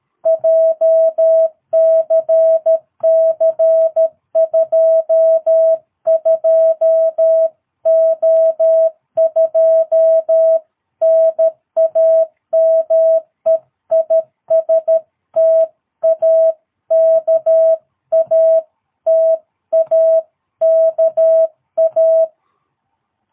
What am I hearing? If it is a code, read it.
Morse code, heard as a keyed tone, about 13 wpm, 660 Hz. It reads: JCC 22O2 NAME IS TAKA TAKA
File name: jh8cbh / 2022cw